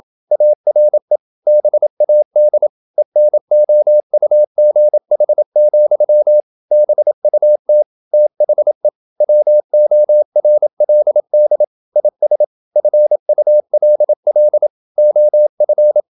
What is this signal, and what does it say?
Morse code, heard as a keyed tone, about 27 wpm, 600 Hz. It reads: ARE BAD ENOUGH, BUT THE WORLD IS FULL OF
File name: Never 0027wpm